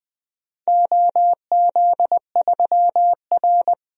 Morse code, heard as a keyed tone, about 20 wpm, 700 Hz. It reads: OZ3R
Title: DXCC-100 Call Signs - 4 Characters - Repeated 3x - 20wpm